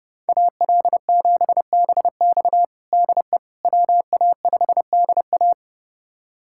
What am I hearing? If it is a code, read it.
Morse code, heard as a keyed tone, about 30 wpm, 700 Hz. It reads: AL7BX DE WA5DA
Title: USA Callsign DE Callsign - 5 Characters - 30wpm